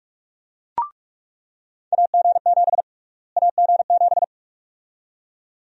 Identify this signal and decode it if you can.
Morse code, heard as a keyed tone, about 43 wpm, 700 Hz. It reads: AG7 AG7
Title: USA Call Sign Prefixes - Repeated 2x - 45wpm